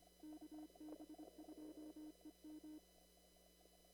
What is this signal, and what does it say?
Morse code, heard as a keyed tone, about 25 wpm, 310 Hz. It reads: KB2EM
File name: CW / kt5j wpx cw-op-k5pi